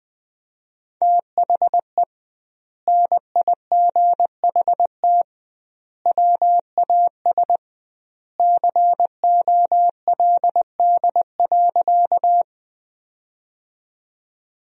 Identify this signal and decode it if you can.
Morse code, heard as a keyed tone, about 20 wpm, 700 Hz. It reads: THE NIGHT WAS COLD.
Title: 2x Word-Spacing - Sentences from Top 400 Words - 20wpm